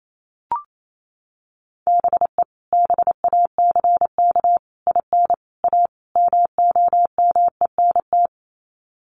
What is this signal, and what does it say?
Morse code, heard as a keyed tone, about 28 wpm, 700 Hz. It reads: BE BACK IN A MOMENT
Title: Sets of 5 Words 28wpm